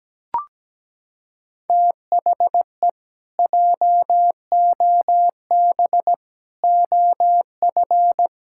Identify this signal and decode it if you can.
Morse code, heard as a keyed tone, about 17 wpm, 700 Hz. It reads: THE JOB OF
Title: Pangram Challenge v2 - 17wpm